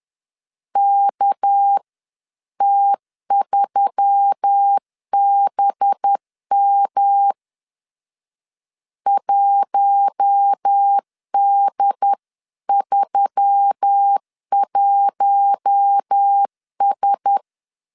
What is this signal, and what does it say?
Morse code, heard as a keyed tone, about 10 wpm, 800 Hz. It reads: KT3BM 1D31S